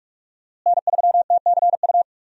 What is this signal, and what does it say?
Morse code, heard as a keyed tone, about 47 wpm, 700 Hz. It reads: N3TCU
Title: USA Call Signs - 5 Characters - Repeated 2x - 45wpm